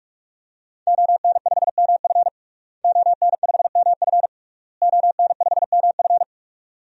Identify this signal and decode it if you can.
Morse code, heard as a keyed tone, about 46 wpm, 700 Hz. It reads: ON5MF ON5MF ON5MF
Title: DXCC-100 Call Signs - 5 Characters - Repeated 3x - 45wpm